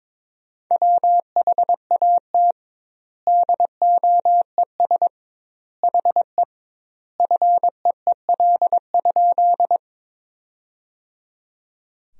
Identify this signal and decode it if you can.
Morse code, heard as a keyed tone, about 22 wpm, 700 Hz. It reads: WHAT DOES HE FEEL?
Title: 2x Word-Spacing - Sentences from Top 400 Words - 22wpm